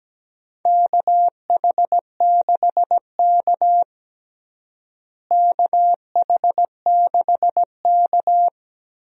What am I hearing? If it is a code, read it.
Morse code, heard as a keyed tone, about 17 wpm, 700 Hz. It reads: KH6K KH6K
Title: DXCC-100 Call Signs - 4 Characters - Repeated 3x - 17wpm